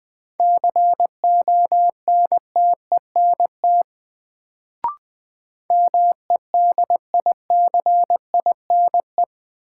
Morse code, heard as a keyed tone, about 21 wpm, 700 Hz. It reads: CONTENT MEDICINE
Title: Sentences from Top 900 Words - New Word Drill - 20wpm